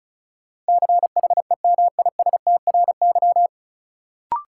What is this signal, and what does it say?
Morse code, heard as a keyed tone, about 36 wpm, 700 Hz. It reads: CHEMISTRY
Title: Sentences from Top 1000 Words - Review Part 6 - New Word Drill - 35wpm